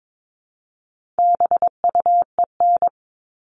Morse code, heard as a keyed tone, about 22 wpm, 700 Hz. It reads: BUEN
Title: All Cut Numbers - 4-Digit 22wpm